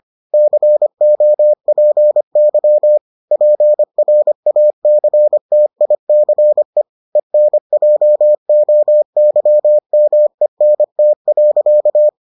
Morse code, heard as a keyed tone, about 25 wpm, 600 Hz. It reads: COPY PRACTICE ENJOYMENT.